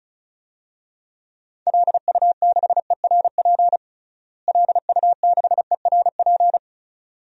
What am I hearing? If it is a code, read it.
Morse code, heard as a keyed tone, about 34 wpm, 700 Hz. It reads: LU6ERP LU6ERP
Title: DXCC-100 Call Signs - 6 Characters - Repeated 2x - 35wpm